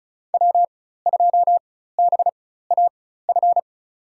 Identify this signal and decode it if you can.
Morse code, heard as a keyed tone, about 36 wpm, spaced as for 16 wpm, 700 Hz. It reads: W2BAF
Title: USA Call Signs - 5 Characters - 4x Character Spacing - 35wpm